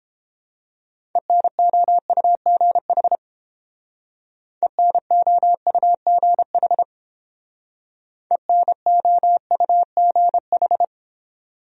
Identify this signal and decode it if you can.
Morse code, heard as a keyed tone, about 30 wpm, 700 Hz. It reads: ENOUGH ENOUGH ENOUGH